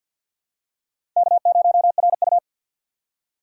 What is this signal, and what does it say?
Morse code, heard as a keyed tone, about 51 wpm, 700 Hz. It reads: K0RU
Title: USA Call Signs - 4 Characters - 6x Character Spacing - 50wpm